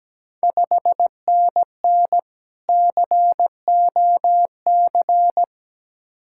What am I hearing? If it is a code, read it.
Morse code, heard as a keyed tone, about 17 wpm, 690 Hz. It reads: 5NN COC
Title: Ontario QSO Party Exchange 17wpm